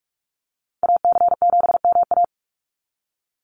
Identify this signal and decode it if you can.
Morse code, heard as a keyed tone, about 46 wpm, 700 Hz. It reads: AC7MA
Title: NA SOTA Activator Callsigns 45wpm